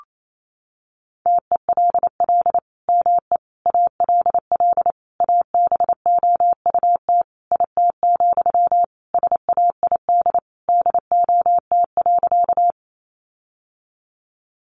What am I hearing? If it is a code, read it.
Morse code, heard as a keyed tone, about 28 wpm, 700 Hz. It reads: TELL ME ALL ABOUT IT, SAID DOT.
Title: Sentences from Top 1500 Words 28wpm